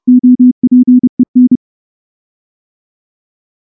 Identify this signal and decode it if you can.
Morse code, heard as a keyed tone, about 29 wpm, 250 Hz. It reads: OPEN
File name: morse / morse-training-30WPM